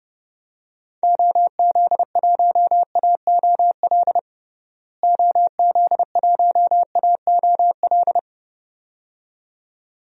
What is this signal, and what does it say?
Morse code, heard as a keyed tone, about 30 wpm, 700 Hz. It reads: OZ1AOL OZ1AOL
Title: DXCC-100 Call Signs - 6 Characters - Repeated 2x - 30wpm